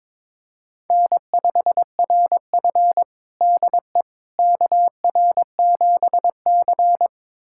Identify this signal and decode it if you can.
Morse code, heard as a keyed tone, about 22 wpm, 700 Hz. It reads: N5RF DE KR7C
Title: USA Callsign DE Callsign - 4 Characters - 22wpm